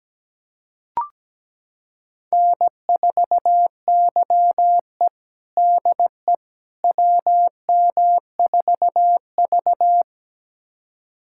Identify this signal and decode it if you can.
Morse code, heard as a keyed tone, about 17 wpm, 700 Hz. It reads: N4YE DE WM4V
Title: USA Callsign DE Callsign - 4 Characters - 17wpm